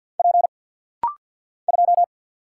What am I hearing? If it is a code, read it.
Morse code, heard as a keyed tone, about 50 wpm, 690 Hz. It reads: P 2
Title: Single Letter-Number Alt - Rapid-Fire 50wpm